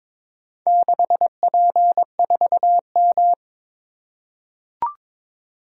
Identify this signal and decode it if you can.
Morse code, heard as a keyed tone, about 22 wpm, 700 Hz. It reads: 6P4M
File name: practice / 22wpm / Four Letter-Number Combinations - 22wpm